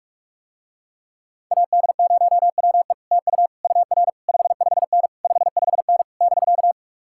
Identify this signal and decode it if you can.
Morse code, heard as a keyed tone, about 49 wpm, 700 Hz. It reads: AD0WE TU UR 55N 55N <BK>